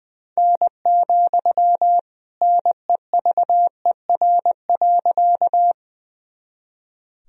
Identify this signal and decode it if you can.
Morse code, heard as a keyed tone, about 20 wpm, 700 Hz. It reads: N, NEVER.
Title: Sentences from Top 300 Words and Abbreviations 20wpm